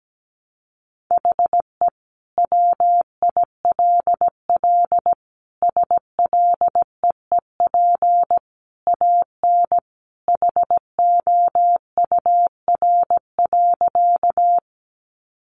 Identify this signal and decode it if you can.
Morse code, heard as a keyed tone, about 17 wpm, 700 Hz. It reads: HE WILL SLEEP AN HOUR.
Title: Sentences from Top 500 Words 17wpm